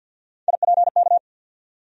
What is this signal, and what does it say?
Morse code, heard as a keyed tone, about 49 wpm, 700 Hz. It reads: IPK
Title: USA Call Sign Postfixes - 3 Letters 50wpm